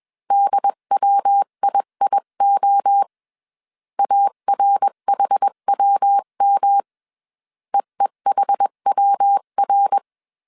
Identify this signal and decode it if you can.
Morse code, heard as a keyed tone, about 21 wpm, 800 Hz. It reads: DWIIO ARHWM EEHWR
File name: 20wpm